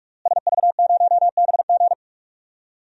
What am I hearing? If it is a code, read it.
Morse code, heard as a keyed tone, about 46 wpm, 700 Hz. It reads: SV0BG
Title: DXCC-100 Call Signs - 5 Characters - Repeated 3x - 45wpm